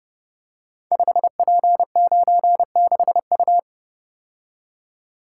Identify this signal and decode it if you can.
Morse code, heard as a keyed tone, about 30 wpm, 700 Hz. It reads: 5P96U
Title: Code Groups - 5 Letter:Numbers - 30wpm